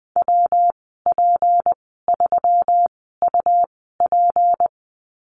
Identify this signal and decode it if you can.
Morse code, heard as a keyed tone, about 20 wpm, spaced as for 15 wpm, 700 Hz. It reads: WP3UP
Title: 193 Lesson 31g - Letter v - Call Signs 20wpm